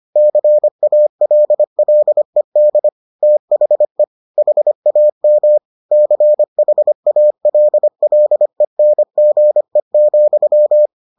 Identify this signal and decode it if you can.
Morse code, heard as a keyed tone, about 25 wpm, 600 Hz. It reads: CALLED THE HAM CHALLENGE,